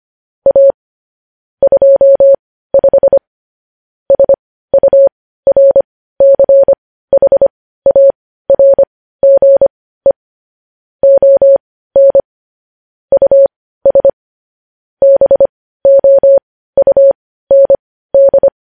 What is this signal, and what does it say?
Morse code, heard as a keyed tone, about 25 wpm, spaced as for 15 wpm, 550 Hz. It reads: A 25 SURCHARGE ON US BOUND
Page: News headlines in Morse code at 15 wpm